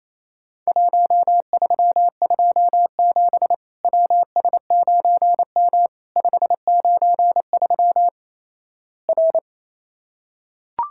Morse code, heard as a keyed tone, about 28 wpm, 700 Hz. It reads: 1327 WS9M 593 R
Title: Worked All Europe DX Contest - QTC Exchange - 28wpm